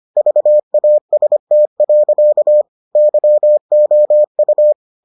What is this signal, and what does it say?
Morse code, heard as a keyed tone, about 25 wpm, 600 Hz. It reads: VAST. YOU